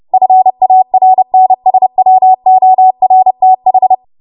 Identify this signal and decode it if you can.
Morse code, heard as a keyed tone, about 30 wpm, 750 Hz. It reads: FARNSWORTH
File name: Morse / 30WPM